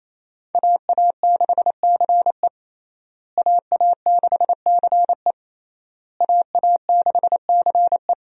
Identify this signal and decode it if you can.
Morse code, heard as a keyed tone, about 27 wpm, 700 Hz. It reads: AA6CE AA6CE AA6CE
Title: USA Call Signs - 5 Characters - Repeated 3x - 28wpm